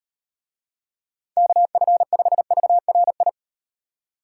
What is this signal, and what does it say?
Morse code, heard as a keyed tone, about 38 wpm, 700 Hz. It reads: KF5VRI